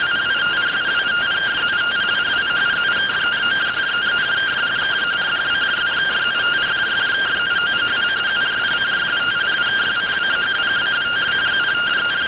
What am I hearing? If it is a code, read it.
Two-tone teleprinter signal, about 170 Hz shift.